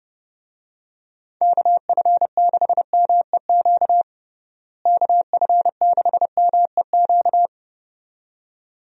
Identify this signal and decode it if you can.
Morse code, heard as a keyed tone, about 31 wpm, 700 Hz. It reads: KF6MEQ KF6MEQ
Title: USA Call Signs - 6 Characters - Repeated 2x - 30wpm